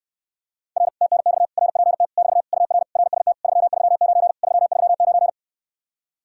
Morse code, heard as a keyed tone, about 69 wpm, 700 Hz. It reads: R TNX RPT <BT> UR RST 339 339